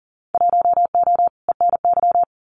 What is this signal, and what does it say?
Morse code, heard as a keyed tone, about 40 wpm, 700 Hz. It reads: 1O ENY
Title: Winter Field Day Exchanges 40wpm